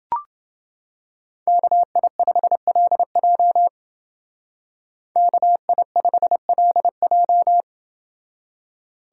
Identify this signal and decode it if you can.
Morse code, heard as a keyed tone, about 29 wpm, 700 Hz. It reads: KI5LJ KI5LJ